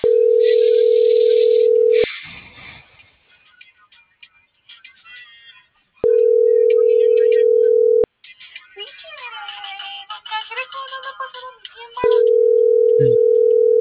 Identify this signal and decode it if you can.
Ringback tone (North America), 440 + 480 Hz.